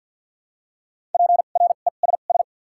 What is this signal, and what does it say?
Morse code, heard as a keyed tone, about 54 wpm, spaced as for 37 wpm, 700 Hz. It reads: PRESS
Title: 2x Character Spacing - Top 300 Words - 50wpm